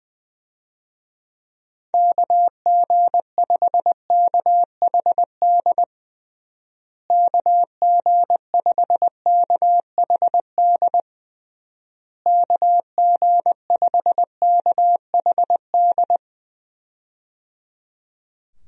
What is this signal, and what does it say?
Morse code, heard as a keyed tone, about 20 wpm, 700 Hz. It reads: KG5KHD KG5KHD KG5KHD
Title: USA Call Signs - 6 Characters - Repeated 3x - 20wpm